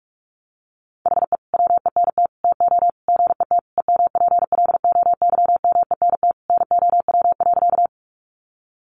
Morse code, heard as a keyed tone, about 44 wpm, 700 Hz. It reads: HE WENT TO GET EMPLOYMENT NOW.
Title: Sentences from Top 1500 Words 45wpm